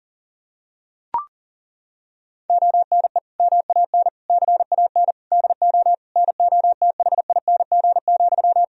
Morse code, heard as a keyed tone, about 40 wpm, 700 Hz. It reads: ONE MAN CAN DO NOTHING,
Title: Sentences from Top 1000 Words - Review Part 4 - 40wpm